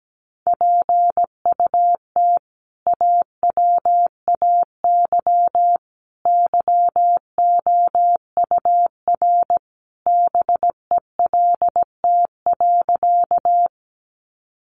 Morse code, heard as a keyed tone, about 17 wpm, 700 Hz. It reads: PUT AWAY YOUR BELT.
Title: Sentences from Top 1400 Words 17wpm